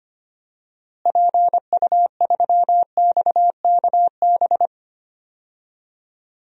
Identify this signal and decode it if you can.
Morse code, heard as a keyed tone, about 25 wpm, 700 Hz. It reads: PU3XKB
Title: DXCC-100 Call Signs - 6 Characters - Repeated 2x - 25wpm